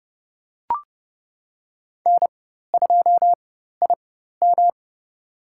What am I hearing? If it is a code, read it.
Morse code, heard as a keyed tone, about 30 wpm, spaced as for 14 wpm, 700 Hz. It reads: N2IM
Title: USA Call Signs - 4 Characters - 4x Character Spacing - 30wpm